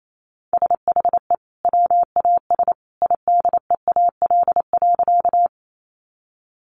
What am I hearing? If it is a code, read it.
Morse code, heard as a keyed tone, about 28 wpm, 700 Hz. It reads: SHE WAS IDEAL.
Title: Sentences from Top 1400 Words 28wpm